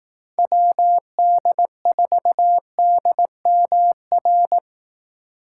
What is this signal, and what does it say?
Morse code, heard as a keyed tone, about 19 wpm, 700 Hz. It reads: WD4DMR
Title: Speed Racing - USA 2x3 Call Signs 15wpm